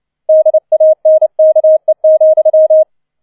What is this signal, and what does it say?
Morse code, heard as a keyed tone, about 29 wpm, 630 Hz. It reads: DANKE,